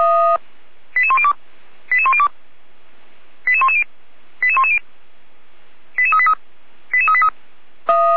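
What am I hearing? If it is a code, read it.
Five-tone selective call (ZVEI), heard as quick runs of five tones, tones of about 70 ms. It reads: 88102 88102 88110 88110 88383 88383
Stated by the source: ZVEI five-tone selective call received on SDR, from the Feuerwehr Orlamünde audio server.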